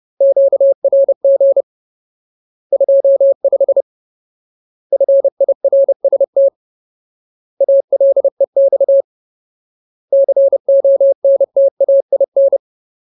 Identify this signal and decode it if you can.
Morse code, heard as a keyed tone, about 30 wpm, 550 Hz. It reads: QRG 25 FIRST ALEX CONTAIN
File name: Ultimate Word List / 30WPM Ultimate